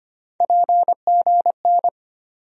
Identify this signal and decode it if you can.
Morse code, heard as a keyed tone, about 25 wpm, 700 Hz. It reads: PGN
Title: USA Call Sign Postfixes - 3 Letters 25wpm